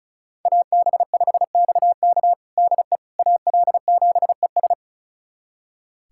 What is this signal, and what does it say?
Morse code, heard as a keyed tone, about 36 wpm, 700 Hz. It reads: AB5XK DE AL7ES
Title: USA Callsign DE Callsign - 5 Characters - 35wpm